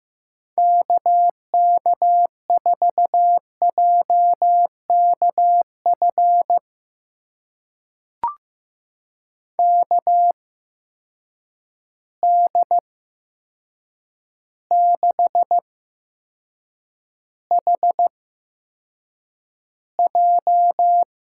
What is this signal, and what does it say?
Morse code, heard as a keyed tone, about 15 wpm, spaced as for 7 wpm, 700 Hz. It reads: KK4JKF KD6HJ
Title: USA Call Signs - 6 Characters - 8x Character Spacing - 15wpm